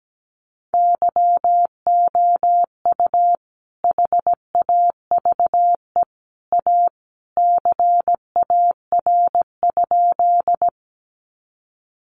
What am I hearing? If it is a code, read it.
Morse code, heard as a keyed tone, about 17 wpm, 700 Hz. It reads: YOU HAVE A CAR?
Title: Sentences from Top 300 Words 17wpm